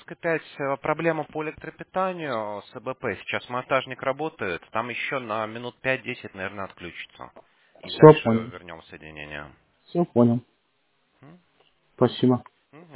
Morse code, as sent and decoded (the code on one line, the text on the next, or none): none